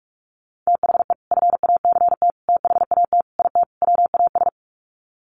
Morse code, heard as a keyed tone, about 44 wpm, 700 Hz. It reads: THE FACT THAT IT WAS
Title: Sets of 5 Words Encore2 45wpm